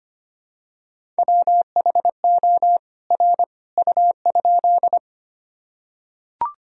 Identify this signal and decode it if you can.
Morse code, heard as a keyed tone, about 25 wpm, 700 Hz. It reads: WHO R U?
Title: Sentences from Top 100 Words and Abbreviations 25wpm